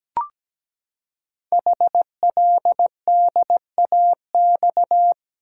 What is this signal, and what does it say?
Morse code, heard as a keyed tone, about 17 wpm, 700 Hz. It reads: HLDAX
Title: Code Groups - 5 Letters - 17wpm